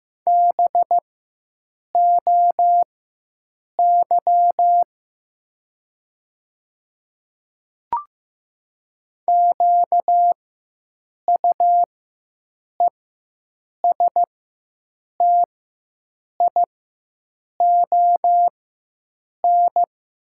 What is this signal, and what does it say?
Morse code, heard as a keyed tone, about 15 wpm, spaced as for 7 wpm, 700 Hz. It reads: BOY QUESTION
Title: 4x Character Spacing - Top 500 Words - 15wpm